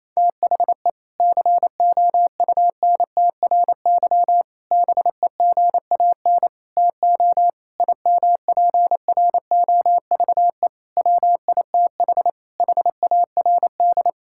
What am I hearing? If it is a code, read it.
Morse code, heard as a keyed tone, about 27 wpm, 700 Hz. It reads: THE COUNTRY BEGAN TO IMPROVE WITH HARD
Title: Sentences from Top 1000 Words - Review Part 5 - 28wpm